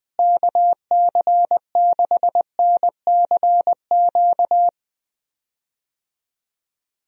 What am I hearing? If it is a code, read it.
Morse code, heard as a keyed tone, about 21 wpm, 700 Hz. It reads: KC6NCQ
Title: USA Call Signs - 6 Characters - Repeated 2x - 20wpm